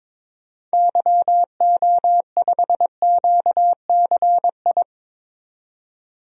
Morse code, heard as a keyed tone, about 22 wpm, 700 Hz. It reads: YO5QCI
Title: DXCC-100 Call Signs - 6 Characters - 22wpm